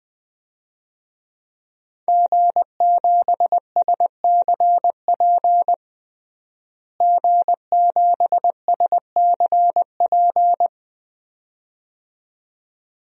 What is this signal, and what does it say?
Morse code, heard as a keyed tone, about 20 wpm, 700 Hz. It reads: G7SCP G7SCP
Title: DXCC-100 Call Signs - 5 Characters - Repeated 2x - 20wpm